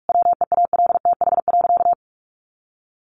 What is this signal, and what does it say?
Morse code, heard as a keyed tone, about 44 wpm, 700 Hz. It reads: WEALTH.
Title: Sentences from Top 1200 Words 45wpm